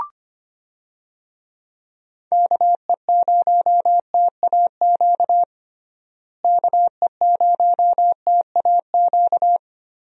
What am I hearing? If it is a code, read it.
Morse code, heard as a keyed tone, about 25 wpm, 700 Hz. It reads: KE0TAQ KE0TAQ
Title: USA Call Signs - 6 Characters - Repeated 2x - 25wpm